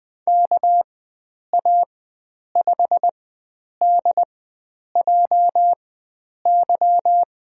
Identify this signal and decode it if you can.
Morse code, heard as a keyed tone, about 20 wpm, spaced as for 9 wpm, 700 Hz. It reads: KA5DJY
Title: USA Call Signs - 6 Characters - 4x Character Spacing - 20wpm